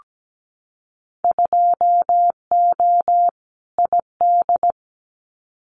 Morse code, heard as a keyed tone, about 17 wpm, 700 Hz. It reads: 2O ID
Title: Winter Field Day Exchanges 17wpm